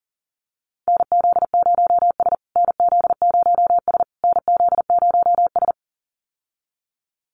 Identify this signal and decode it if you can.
Morse code, heard as a keyed tone, about 40 wpm, 700 Hz. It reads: NZ0S NZ0S NZ0S
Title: USA Call Signs Repeated 3x 40wpm